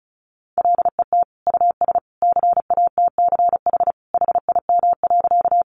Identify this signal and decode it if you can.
Morse code, heard as a keyed tone, about 34 wpm, 700 Hz. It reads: LET US CATCH HIM.
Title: Sentences from Top 1500 Words 35wpm